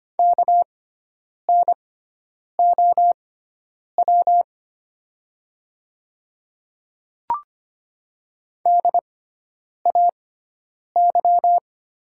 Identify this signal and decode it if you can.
Morse code, heard as a keyed tone, about 25 wpm, spaced as for 9 wpm, 700 Hz. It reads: KNOW DAY
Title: 6x Character Spacing - Top 100 Words - 25wpm